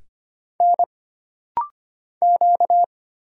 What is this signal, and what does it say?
Morse code, heard as a keyed tone, about 25 wpm, 700 Hz. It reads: N Q
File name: Single Character (Letter:Number:Punctuation:Prosign) - Rapid-Fire - 25wpm